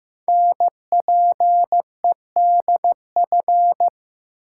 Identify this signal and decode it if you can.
Morse code, heard as a keyed tone, about 15 wpm, 700 Hz. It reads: NPEDF
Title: Code Groups - 5 Letter:Numbers - 15wpm